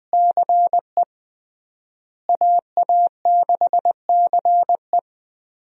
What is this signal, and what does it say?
Morse code, heard as a keyed tone, about 20 wpm, 700 Hz. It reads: CE AA6CE
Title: USA Call Signs - 5 Characters - Repeated 3x - 20wpm